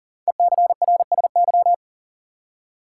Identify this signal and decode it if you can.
Morse code, heard as a keyed tone, about 40 wpm, 700 Hz. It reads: ECRSY
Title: Code Groups - 5 Letters - 40wpm